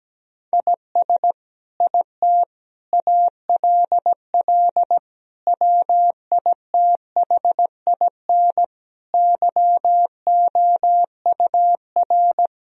Morse code, heard as a keyed tone, about 17 wpm, 700 Hz. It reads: IS IT ALL WITHIN YOUR
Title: Sentences from Top 1000 Words - Review Part 2 - 17wpm